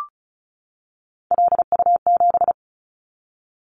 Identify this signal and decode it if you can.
Morse code, heard as a keyed tone, about 36 wpm, 700 Hz. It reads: LU7
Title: Callsign Prefixes for Top 100 DXCC - 35wpm